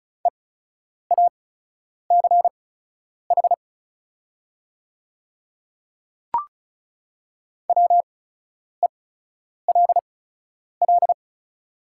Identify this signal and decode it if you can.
Morse code, heard as a keyed tone, about 36 wpm, spaced as for 10 wpm, 700 Hz. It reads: EACH WELL
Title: 8x Character Spacing - Top 300 Words - 35wpm